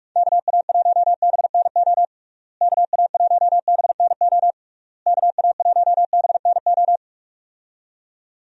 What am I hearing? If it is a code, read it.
Morse code, heard as a keyed tone, about 44 wpm, 700 Hz. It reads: KA1BNO KA1BNO KA1BNO
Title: USA Call Signs - 6 Characters - Repeated 3x - 45wpm